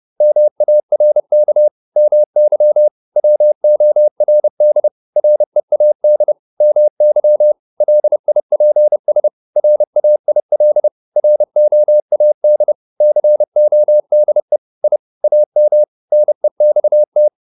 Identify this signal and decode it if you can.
Morse code, heard as a keyed tone, about 30 wpm, 600 Hz. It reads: MARK MY WORD READ MY LIPS RAIL ROAD CODE I AM NEXT